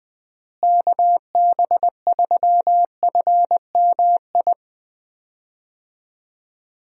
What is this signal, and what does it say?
Morse code, heard as a keyed tone, about 20 wpm, 700 Hz. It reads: KB3FMI
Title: USA Call Signs - 6 Characters - Repeated 2x - 20wpm